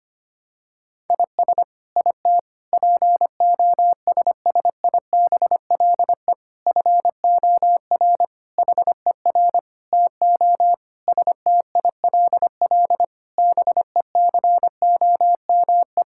Morse code, heard as a keyed tone, about 25 wpm, 700 Hz. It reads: IS IT POSSIBLE FOR HER TO STILL BECOME
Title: Sentences from Top 1000 Words - Review Part 6 - 25wpm